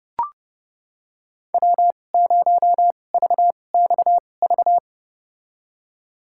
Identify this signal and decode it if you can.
Morse code, heard as a keyed tone, about 30 wpm, spaced as for 22 wpm, 700 Hz. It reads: W0VXV
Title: USA Call Signs - 5 Characters - 2x Character Spacing - 30wpm